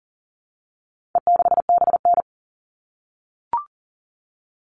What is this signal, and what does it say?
Morse code, heard as a keyed tone, about 40 wpm, 700 Hz. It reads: E6BN
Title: All Cut Numbers - 4-Digit 40wpm